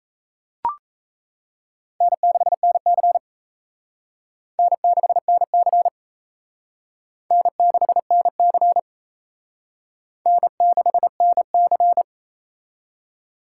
Morse code, heard as a keyed tone, about 37 wpm, 700 Hz. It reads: N6NC N6NC N6NC N6NC